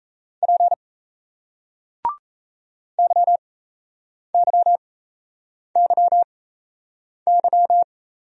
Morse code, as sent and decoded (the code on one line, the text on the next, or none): .--. / -.-- -.-- -.-- -.--
P YYYY